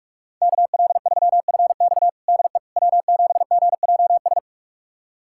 Morse code, heard as a keyed tone, about 46 wpm, 700 Hz. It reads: KL3FX DE W7GJS
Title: USA Callsign DE Callsign - 5 Characters - 45wpm